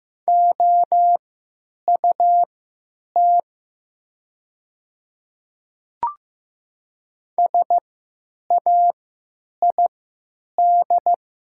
Morse code, heard as a keyed tone, about 15 wpm, spaced as for 9 wpm, 700 Hz. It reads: OUT SAID
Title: 3x Character Spacing - Top 100 Words - 15wpm